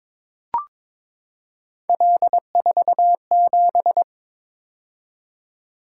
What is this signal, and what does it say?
Morse code, heard as a keyed tone, about 23 wpm, 700 Hz. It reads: L47
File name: practice / 22wpm / Three Letter-Number Combinations - 22wpm